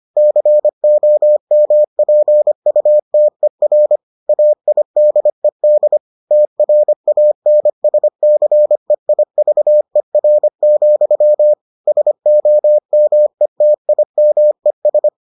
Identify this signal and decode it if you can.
Morse code, heard as a keyed tone, about 25 wpm, 600 Hz. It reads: COMPUTER AIDED TRANSCEIVER, SOMETIMES